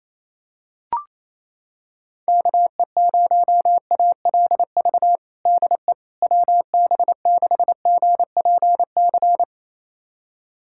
Morse code, heard as a keyed tone, about 28 wpm, 700 Hz. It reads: KE0ALV DE WB6GPC
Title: USA Callsign DE Callsign - 6 Characters - 28wpm